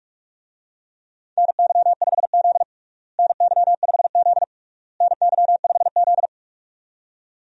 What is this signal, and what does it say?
Morse code, heard as a keyed tone, about 46 wpm, 700 Hz. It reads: NY5Z NY5Z NY5Z
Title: USA Call Signs - 4 Characters - Repeated 3x - 45wpm